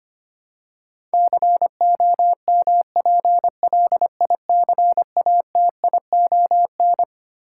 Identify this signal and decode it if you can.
Morse code, heard as a keyed tone, about 25 wpm, 700 Hz. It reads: COMPLICATION
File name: Twelve Letter Words - 25wpm